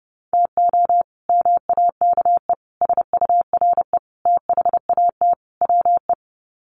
Morse code, heard as a keyed tone, about 30 wpm, 700 Hz. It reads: TO MAKE SURE THAT WE
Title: Sets of 5 Words Encore 30wpm